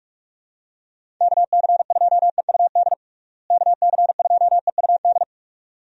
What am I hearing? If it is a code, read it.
Morse code, heard as a keyed tone, about 46 wpm, 700 Hz. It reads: KC2EUD KC2EUD
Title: USA Call Signs - 6 Characters - Repeated 2x - 45wpm